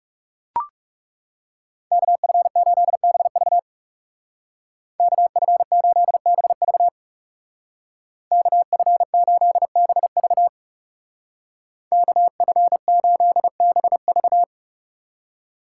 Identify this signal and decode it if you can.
Morse code, heard as a keyed tone, about 35 wpm, 700 Hz. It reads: KF8BV KF8BV KF8BV KF8BV
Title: Speed Racing - USA 2x2 Call Signs 30wpm